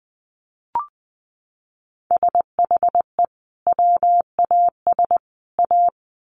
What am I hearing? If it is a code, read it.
Morse code, heard as a keyed tone, about 20 wpm, 700 Hz. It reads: SHE WAS A
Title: Sets of 3 Words 20wpm